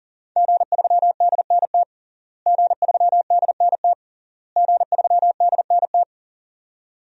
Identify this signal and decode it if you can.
Morse code, heard as a keyed tone, about 37 wpm, 700 Hz. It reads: G3DNT G3DNT G3DNT
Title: DXCC-100 Call Signs - 5 Characters - Repeated 3x - 40wpm